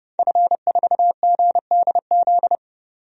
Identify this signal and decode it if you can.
Morse code, heard as a keyed tone, about 32 wpm, 700 Hz. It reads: F4GDZ
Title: DXCC-100 Call Signs - 5 Characters - 30wpm